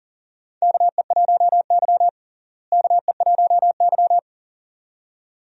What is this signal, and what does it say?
Morse code, heard as a keyed tone, about 38 wpm, 700 Hz. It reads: KE1Y KE1Y
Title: USA Call Signs - 4 Characters - Repeated 2x - 40wpm